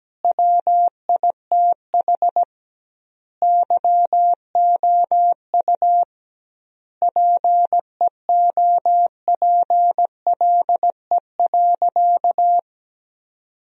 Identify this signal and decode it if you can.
Morse code, heard as a keyed tone, about 17 wpm, 700 Hz. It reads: WITH YOU PEOPLE.
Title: 2x Word-Spacing - Sentences from Top 300 Words - 17wpm